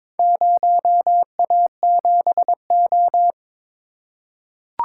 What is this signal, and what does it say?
Morse code, heard as a keyed tone, about 22 wpm, 700 Hz. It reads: 0A7O
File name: practice / 22wpm / Four Letter-Number Combinations - 22wpm